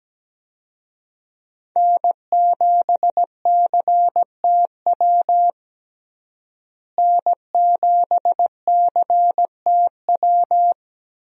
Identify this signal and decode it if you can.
Morse code, heard as a keyed tone, about 17 wpm, 700 Hz. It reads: N7CTW N7CTW
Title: USA Call Signs - 5 Characters - Repeated 3x - 17wpm